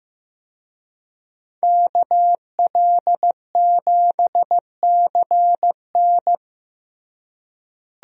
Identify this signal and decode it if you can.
Morse code, heard as a keyed tone, about 15 wpm, 700 Hz. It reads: KL7CN
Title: USA Call Signs - 5 Characters - Repeated 2x - 15wpm